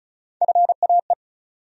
Morse code, heard as a keyed tone, about 36 wpm, 700 Hz. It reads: FAE
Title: USA Call Sign Postfixes - 3 Letters 35wpm